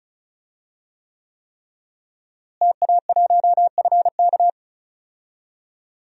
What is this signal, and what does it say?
Morse code, heard as a keyed tone, about 34 wpm, 700 Hz. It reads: TA1FK
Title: DXCC-100 Call Signs - 5 Characters - Repeated 3x - 35wpm